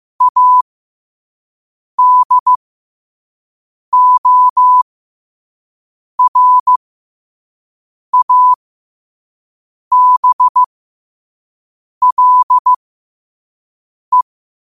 Morse code, heard as a keyed tone, about 15 wpm, spaced as for 5 wpm, 1000 Hz. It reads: ADORABLE
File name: qotdfiles / 5wpm